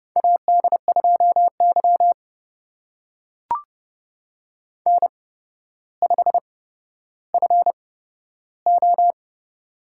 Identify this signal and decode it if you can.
Morse code, heard as a keyed tone, about 30 wpm, spaced as for 13 wpm, 700 Hz. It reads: AD2Y N5FO
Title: USA Call Signs - 4 Characters - 8x Character Spacing - 30wpm